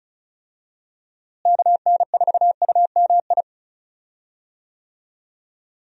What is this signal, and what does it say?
Morse code, heard as a keyed tone, about 34 wpm, 700 Hz. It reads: KN4UMI
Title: USA Call Signs - 6 Characters - Repeated 3x - 35wpm